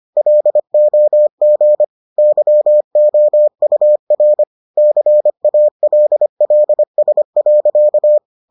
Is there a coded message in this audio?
Yes: Morse code from 0.2 to 8.2 s, heard as a keyed tone, about 25 wpm, 600 Hz. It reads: LOG YOUR CALLS.